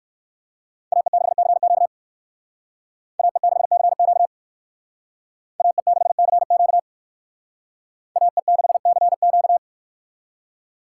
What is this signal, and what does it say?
Morse code, heard as a keyed tone, about 58 wpm, 700 Hz. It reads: AE6CQ AE6CQ AE6CQ AE6CQ